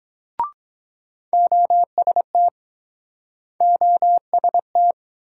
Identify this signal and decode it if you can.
Morse code, heard as a keyed tone, about 23 wpm, 700 Hz. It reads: OST OST